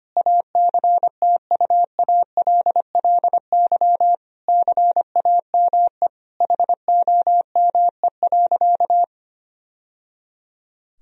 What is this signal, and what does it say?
Morse code, heard as a keyed tone, about 25 wpm, 700 Hz. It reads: ACTUALLY CAME HOME.
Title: Sentences from Top 1000 Words - Review Part 4 - 25wpm